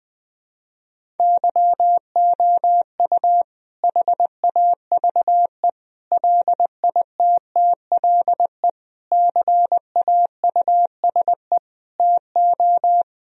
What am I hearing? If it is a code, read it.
Morse code, heard as a keyed tone, about 20 wpm, 700 Hz. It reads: YOU HAVE LITTLE CAUSE TO
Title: Sentences from Top 1000 Words - Review Part 1 - 20wpm